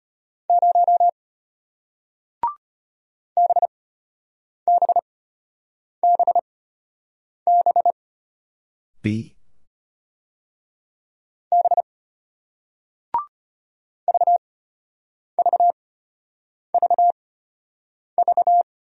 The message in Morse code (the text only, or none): 0 BBBB B VVVV